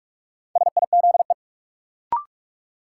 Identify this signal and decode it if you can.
Morse code, heard as a keyed tone, about 44 wpm, 700 Hz. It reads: SIZE
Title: Sets of 3 Words - New Word Drill - 45wpm